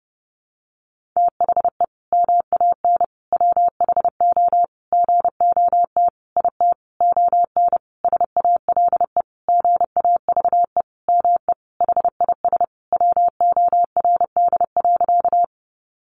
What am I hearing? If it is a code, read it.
Morse code, heard as a keyed tone, about 30 wpm, 700 Hz. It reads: THE MAN WHO GOT IT ON SALE GAVE ME HIS WORD.
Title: Sentences from Top 1400 Words 30wpm